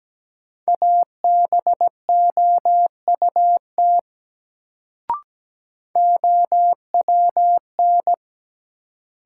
Morse code, heard as a keyed tone, about 17 wpm, 700 Hz. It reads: ABOUT OWN
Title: Sets of 2 Words - New Word Drill - 17wpm